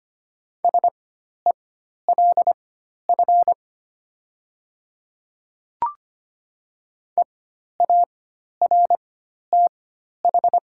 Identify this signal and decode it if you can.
Morse code, heard as a keyed tone, about 25 wpm, spaced as for 12 wpm, 700 Hz. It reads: SELF EARTH
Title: 4x Character Spacing - Top 300 Words - 25wpm